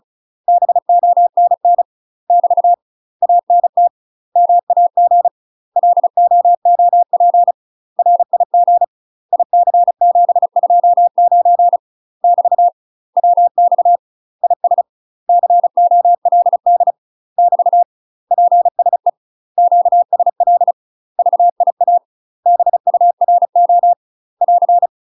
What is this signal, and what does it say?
Morse code, heard as a keyed tone, about 36 wpm, 700 Hz. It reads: BONN <BT> ANT MAG LOOP RIG IC729 <BT> WX IS COLD <BT> PSE QSL VIA BURO <AR>
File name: UK MASTER QSOs TRAINING / 35WPM / UK MASTER QSOs 0023